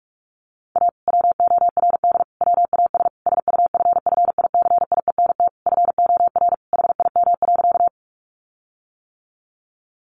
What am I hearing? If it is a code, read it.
Morse code, heard as a keyed tone, about 44 wpm, 700 Hz. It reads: A WORD WAS SUFFICIENT FOR HIM.
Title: Sentences from Top 1400 Words 45wpm